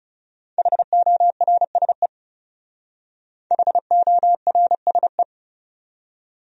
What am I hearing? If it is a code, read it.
Morse code, heard as a keyed tone, about 37 wpm, 700 Hz. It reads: HORSE HORSE
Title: Speed-Racing Top 500 Words 30wpm